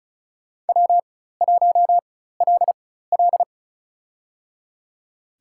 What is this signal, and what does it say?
Morse code, heard as a keyed tone, about 35 wpm, spaced as for 16 wpm, 700 Hz. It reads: W1LL
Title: USA Call Signs - 4 Characters - 4x Character Spacing - 35wpm